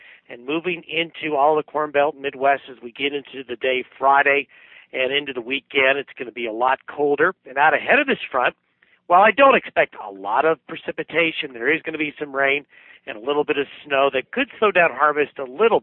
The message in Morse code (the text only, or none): none